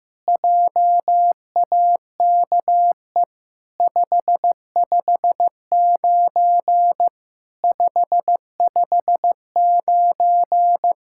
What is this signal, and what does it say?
Morse code, heard as a keyed tone, about 15 wpm, 700 Hz. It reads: JAKE 559 559
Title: SOTA QSO Practice 15wpm